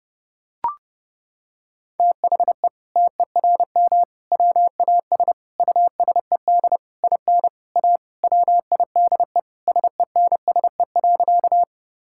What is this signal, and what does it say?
Morse code, heard as a keyed tone, about 30 wpm, 700 Hz. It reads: THE TERM WAS USED IN A WIDE SENSE.
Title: Sentences from Top 1000 Words - Review Part 6 - 30wpm